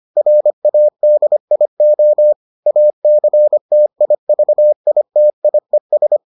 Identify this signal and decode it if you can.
Morse code, heard as a keyed tone, about 25 wpm, 600 Hz. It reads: RADIO ACTIVITIES